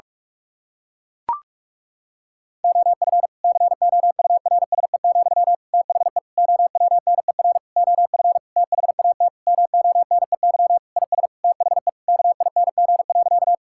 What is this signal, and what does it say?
Morse code, heard as a keyed tone, about 41 wpm, 700 Hz. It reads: OF COURSE, THE OWNER OF THAT MONEY IS THE KING.
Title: Sentences from Top 1600 Words 45wpm